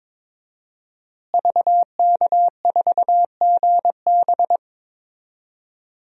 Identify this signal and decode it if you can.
Morse code, heard as a keyed tone, about 21 wpm, 700 Hz. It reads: VK4GB
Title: DXCC-100 Call Signs - 5 Characters - Repeated 2x - 22wpm